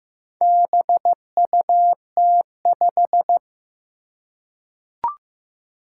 Morse code, heard as a keyed tone, about 15 wpm, 700 Hz. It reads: BUT5
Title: Four Challenging Letter-Number Combinations 15wpm